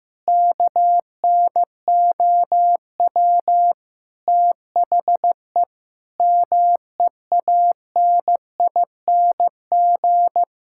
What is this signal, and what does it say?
Morse code, heard as a keyed tone, about 15 wpm, 700 Hz. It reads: KNOW THE MEANING
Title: Sentences from Top 1000 Words - Review Part 6 - 15wpm